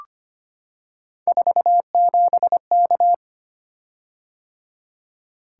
Morse code, heard as a keyed tone, about 25 wpm, 700 Hz. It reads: 47K
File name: Three Letter-Number Combinations - 25wpm